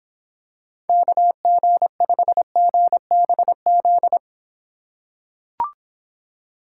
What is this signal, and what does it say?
Morse code, heard as a keyed tone, about 26 wpm, 700 Hz. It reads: KG5GBZ